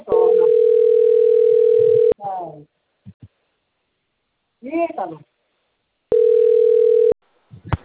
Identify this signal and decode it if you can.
Ringback tone (North America), 440 + 480 Hz, with speech.
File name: out- - - - - .29105